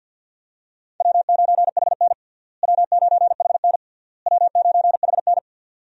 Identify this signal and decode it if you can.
Morse code, heard as a keyed tone, about 52 wpm, 700 Hz. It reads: W9HN W9HN W9HN
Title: USA Call Signs - 4 Characters - Repeated 3x - 50wpm